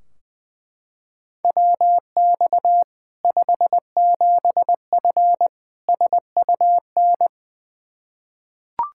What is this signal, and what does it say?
Morse code, heard as a keyed tone, about 20 wpm, 700 Hz. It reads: WX 57F SUN
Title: K1USN SST QSO Practice 20wpm